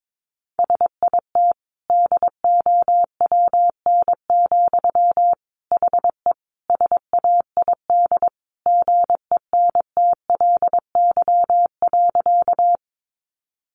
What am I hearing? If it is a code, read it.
Morse code, heard as a keyed tone, about 22 wpm, 700 Hz. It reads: SIT DOWN, HE SAID GENTLY.
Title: Sentences from Top 1500 Words 22wpm